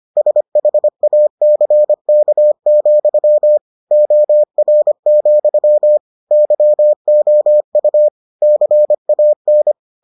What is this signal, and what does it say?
Morse code, heard as a keyed tone, about 25 wpm, 600 Hz. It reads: SHACK, OR, YOU CAN